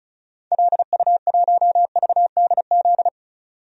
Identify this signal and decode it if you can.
Morse code, heard as a keyed tone, about 34 wpm, 700 Hz. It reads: LU1VDZ
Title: DXCC-100 Call Signs - 6 Characters - 35wpm